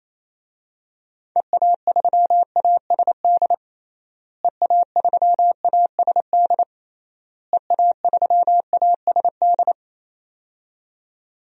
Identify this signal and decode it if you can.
Morse code, heard as a keyed tone, about 28 wpm, 700 Hz. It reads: EA3ASD EA3ASD EA3ASD
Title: DXCC-100 Call Signs - 6 Characters - Repeated 3x - 28wpm